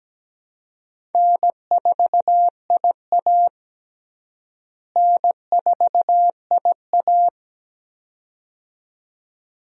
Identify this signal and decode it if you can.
Morse code, heard as a keyed tone, about 17 wpm, 700 Hz. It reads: N4IA N4IA
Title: USA Call Signs - 4 Characters - Repeated 2x - 17wpm